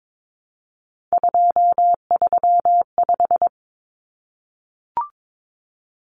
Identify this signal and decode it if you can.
Morse code, heard as a keyed tone, about 22 wpm, 700 Hz. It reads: 235
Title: Common Cut Numbers - 3-Digit 22wpm